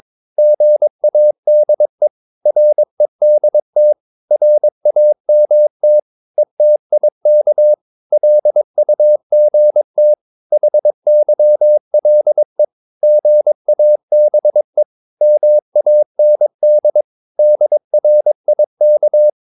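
Morse code, heard as a keyed tone, about 22 wpm, 600 Hz. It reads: GADE REDT RAMT ETIK LUGT HYLE GABE MAND DRIK
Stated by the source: Morse practice of short Danish words at 22 wpm, from CW med Gnister.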